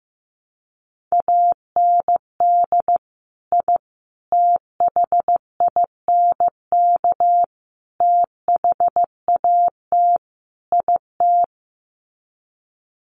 Morse code, heard as a keyed tone, about 15 wpm, 700 Hz. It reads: AND I THINK THAT IT
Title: Sets of 5 Words Encore2 15wpm